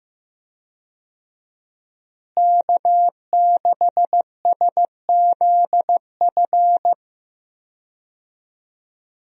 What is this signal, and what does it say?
Morse code, heard as a keyed tone, about 15 wpm, 700 Hz. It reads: K6SZF
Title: USA Call Signs - 5 Characters - Repeated 3x - 15wpm